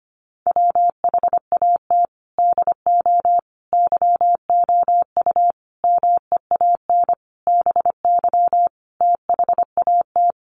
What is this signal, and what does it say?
Morse code, heard as a keyed tone, about 25 wpm, 700 Hz. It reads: WHAT DO YOU MEAN BY THAT
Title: Sentences from Top 1200 Words 25wpm